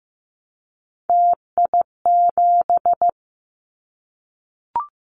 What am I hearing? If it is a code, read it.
Morse code, heard as a keyed tone, about 15 wpm, 700 Hz. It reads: TI7
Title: Callsign Prefixes for Top 100 DXCC - 15wpm